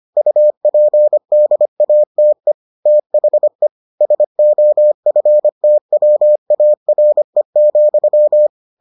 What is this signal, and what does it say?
Morse code, heard as a keyed tone, about 25 wpm, 600 Hz. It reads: UPDATE THE SOFTWARE,